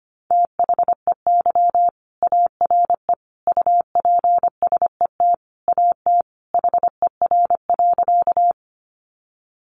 Morse code, heard as a keyed tone, about 25 wpm, 700 Hz. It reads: THEY ARE UPSET AT HER.
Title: Sentences from Top 800 Words 25wpm